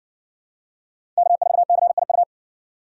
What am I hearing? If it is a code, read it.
Morse code, heard as a keyed tone, about 60 wpm, 700 Hz. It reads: K4CIU